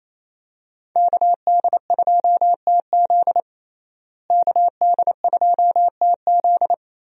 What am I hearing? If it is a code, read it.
Morse code, heard as a keyed tone, about 28 wpm, 700 Hz. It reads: KD2TZ KD2TZ
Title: USA Call Signs - 5 Characters - Repeated 3x - 28wpm